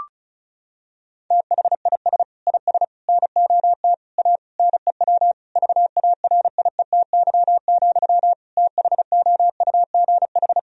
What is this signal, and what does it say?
Morse code, heard as a keyed tone, about 34 wpm, 700 Hz. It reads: THIS IS NOT A NEW VARIETY, THOUGH
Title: Sentences from Top 1000 Words - Review Part 4 - 35wpm